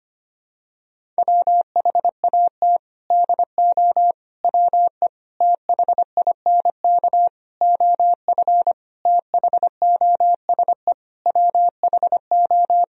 Morse code, heard as a keyed tone, about 25 wpm, 700 Hz. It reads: WHAT DO WE THINK OF THOSE WHO
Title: Sentences from Top 1000 Words - Review Part 2 - 25wpm